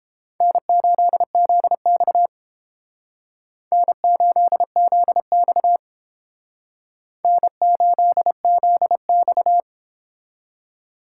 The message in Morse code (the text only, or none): N8ZX N8ZX N8ZX